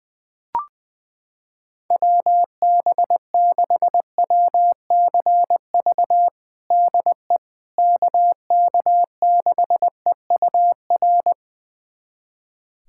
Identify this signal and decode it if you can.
Morse code, heard as a keyed tone, about 20 wpm, 700 Hz. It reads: WB6WCV DE KK6EUR
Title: USA Callsign DE Callsign - 6 Characters - 20wpm